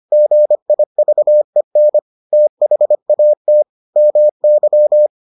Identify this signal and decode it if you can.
Morse code, heard as a keyed tone, about 25 wpm, 600 Hz. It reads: GIVEN THAT MY